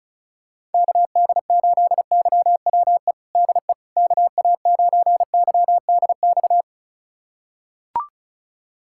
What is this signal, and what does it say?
Morse code, heard as a keyed tone, about 36 wpm, 700 Hz. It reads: KD8YWE DE KA9YDX
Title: USA Callsign DE Callsign 35wpm